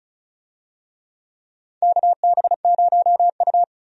Morse code, heard as a keyed tone, about 34 wpm, 700 Hz. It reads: KB0U
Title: USA Call Signs - 4 Characters - 35wpm